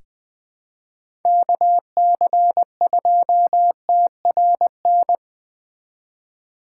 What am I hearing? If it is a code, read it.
Morse code, heard as a keyed tone, about 20 wpm, 700 Hz. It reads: KC2TRN
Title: USA Call Signs - 6 Characters - 4x Character Spacing - 20wpm